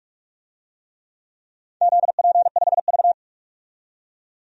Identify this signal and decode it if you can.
Morse code, heard as a keyed tone, about 46 wpm, 700 Hz. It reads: ZP5V
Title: DXCC-100 Call Signs - 4 Characters - 45wpm